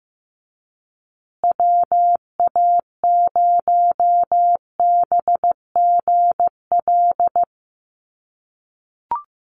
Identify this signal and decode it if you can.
Morse code, heard as a keyed tone, about 15 wpm, 700 Hz. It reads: WA0BGL
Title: USA Call Signs Repeated 3x 15wpm